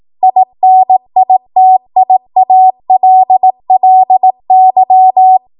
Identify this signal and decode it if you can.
Morse code, heard as a keyed tone, about 18 wpm, 750 Hz. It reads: INITIALLY
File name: Morse / 18WPM